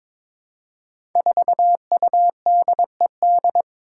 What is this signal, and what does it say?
Morse code, heard as a keyed tone, about 21 wpm, 700 Hz. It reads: 4UDED
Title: Five Challenging Letter-Number Combinations 22wpm